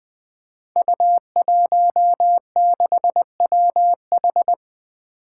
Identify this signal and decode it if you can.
Morse code, heard as a keyed tone, about 20 wpm, 700 Hz. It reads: U16WH
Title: Five Challenging Letter-Number Combinations 20wpm